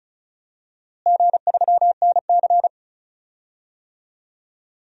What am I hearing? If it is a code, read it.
Morse code, heard as a keyed tone, about 36 wpm, 700 Hz. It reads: G3NC
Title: DXCC-100 Call Signs - 4 Characters - Repeated 3x - 35wpm